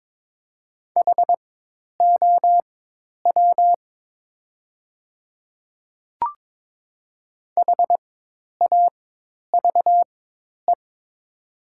Morse code, heard as a keyed tone, about 22 wpm, spaced as for 10 wpm, 700 Hz. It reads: HOW HAVE
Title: 4x Character Spacing - Top 200 Words - 22wpm